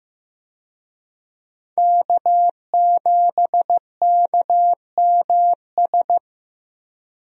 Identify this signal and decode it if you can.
Morse code, heard as a keyed tone, about 15 wpm, 700 Hz. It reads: K7KMS
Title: USA Call Signs - 5 Characters - Repeated 3x - 15wpm